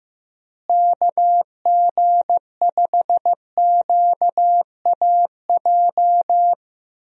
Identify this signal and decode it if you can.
Morse code, heard as a keyed tone, about 15 wpm, 700 Hz. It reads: KG5QAJ
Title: USA Call Signs - 6 Characters - Repeated 2x - 15wpm